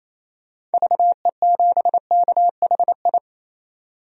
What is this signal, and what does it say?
Morse code, heard as a keyed tone, about 28 wpm, 700 Hz. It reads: VE7KHI
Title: DXCC-100 Call Signs - 6 Characters - 28wpm